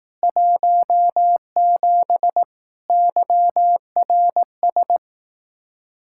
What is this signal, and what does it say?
Morse code, heard as a keyed tone, about 18 wpm, 700 Hz. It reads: 17 YRS
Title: Speed-Racing Common QSO Phrases Encore - 15wpm